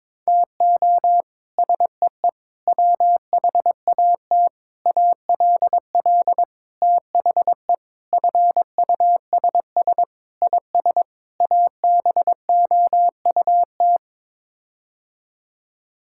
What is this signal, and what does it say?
Morse code, heard as a keyed tone, about 21 wpm, 700 Hz. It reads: TO SEE WHAT ALL THE FUSS IS ABOUT
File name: practice / 22wpm / Sets of 8 Words - 22wpm